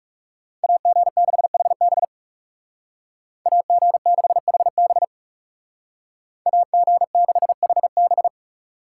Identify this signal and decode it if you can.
Morse code, heard as a keyed tone, about 36 wpm, 700 Hz. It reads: AG6HB AG6HB AG6HB
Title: Speed Racing - USA 2x2 Call Signs 30wpm